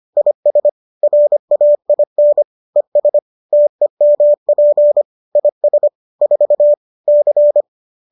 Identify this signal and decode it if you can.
Morse code, heard as a keyed tone, about 25 wpm, 590 Hz. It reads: IS RAIN ES TEMP IS 4 C